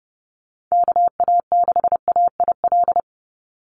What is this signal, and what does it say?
Morse code, heard as a keyed tone, about 30 wpm, 700 Hz. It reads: KA6AIL
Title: Call Signs - 30wpm